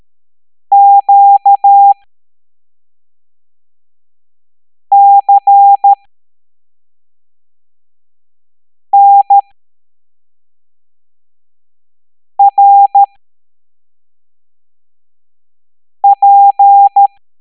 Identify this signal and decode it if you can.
Morse code, heard as a keyed tone, about 13 wpm, spaced as for 3 wpm, 800 Hz. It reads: QCNRP
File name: hamdata / L07-EX02-HQP